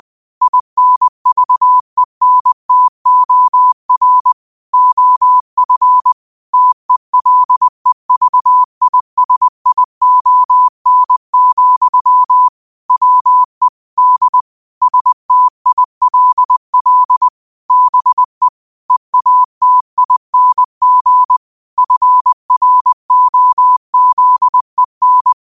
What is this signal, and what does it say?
Morse code, heard as a keyed tone, about 20 wpm, 1000 Hz. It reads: INVENTOR OF TELEVISION, WE D STILL BE EATING FROZEN